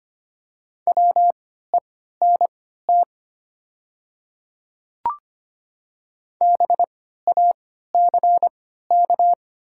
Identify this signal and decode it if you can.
Morse code, heard as a keyed tone, about 25 wpm, spaced as for 14 wpm, 700 Hz. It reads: WENT BACK
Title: 3x Character Spacing - Top 200 Words - 25wpm